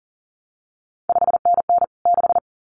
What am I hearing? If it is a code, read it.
Morse code, heard as a keyed tone, about 38 wpm, 700 Hz. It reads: 5NN 6
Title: CQ WW DX Contest Exchange Encore - 40wpm